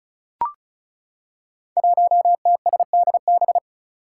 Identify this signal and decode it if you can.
Morse code, heard as a keyed tone, about 36 wpm, 700 Hz. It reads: 1TSDB
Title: Five Challenging Letter-Number Combinations 35wpm